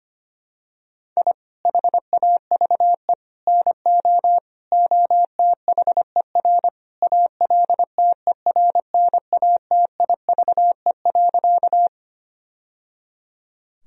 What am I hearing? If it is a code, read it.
Morse code, heard as a keyed tone, about 25 wpm, 700 Hz. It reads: I HAVE NO OTHER ALTERNATIVE.
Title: Sentences from Top 1000 Words - Review Part 4 - 25wpm